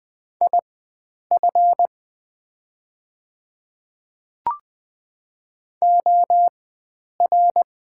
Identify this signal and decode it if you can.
Morse code, heard as a keyed tone, about 20 wpm, spaced as for 9 wpm, 690 Hz. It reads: IF OR
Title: 4x Character Spacing - Top 100 Words - 20wpm